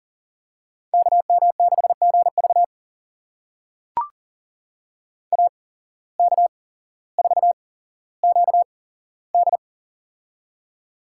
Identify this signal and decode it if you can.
Morse code, heard as a keyed tone, about 40 wpm, spaced as for 17 wpm, 700 Hz. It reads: KM6GV AK4QD
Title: USA Call Signs - 5 Characters - 8x Character Spacing - 40wpm